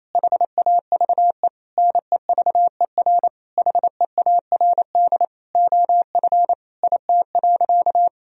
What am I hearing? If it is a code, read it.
Morse code, heard as a keyed tone, about 28 wpm, 700 Hz. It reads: HAVE NEVER HEARD OF IT.
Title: Sentences from Top 1000 Words - Review Part 3 - 28wpm